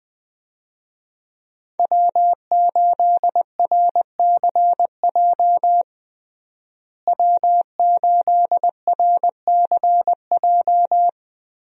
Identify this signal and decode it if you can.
Morse code, heard as a keyed tone, about 20 wpm, 700 Hz. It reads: W8RCJ W8RCJ
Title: USA Call Signs - 5 Characters - Repeated 3x - 20wpm